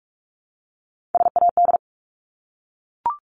Morse code, heard as a keyed tone, about 44 wpm, 700 Hz. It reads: SAD
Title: Top 1501-1750 Words 45wpm